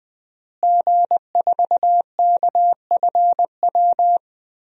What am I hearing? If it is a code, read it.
Morse code, heard as a keyed tone, about 21 wpm, 700 Hz. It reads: G4KFW
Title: DXCC-100 Call Signs - 5 Characters - Repeated 2x - 20wpm